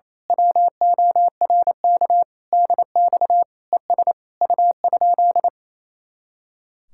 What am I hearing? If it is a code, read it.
Morse code, heard as a keyed tone, about 26 wpm, 700 Hz. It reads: WORK DX ES U?
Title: Speed-Racing Common QSO Phrases - 28wpm